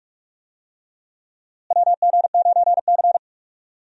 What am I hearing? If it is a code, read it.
Morse code, heard as a keyed tone, about 42 wpm, 700 Hz. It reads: WG9C
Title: USA Call Signs - 4 Characters - Repeated 3x - 45wpm